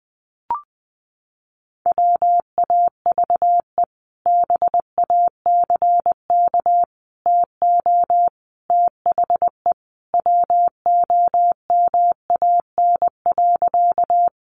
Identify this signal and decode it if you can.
Morse code, heard as a keyed tone, about 20 wpm, 700 Hz. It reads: WAVE BACK TO THE WOMAN.
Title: Sentences from Top 1400 Words 20wpm